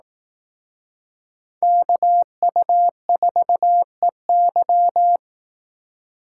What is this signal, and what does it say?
Morse code, heard as a keyed tone, about 18 wpm, 700 Hz. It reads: KU4EY
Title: Speed Racing - USA 2x2 Call Signs 15wpm